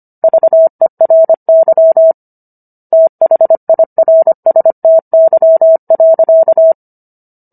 Morse code, heard as a keyed tone, about 25 wpm, 650 Hz. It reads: VERY THIRSTY.